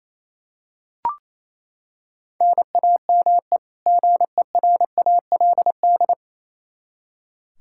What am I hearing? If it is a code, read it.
Morse code, heard as a keyed tone, about 28 wpm, 700 Hz. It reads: NAME GERALD
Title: Practice QSO - Exchange 1 - Beginning - 28wpm